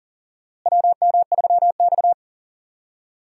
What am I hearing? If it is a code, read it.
Morse code, heard as a keyed tone, about 40 wpm, 700 Hz. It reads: WM3X
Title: USA Call Signs - 4 Characters - Repeated 3x - 40wpm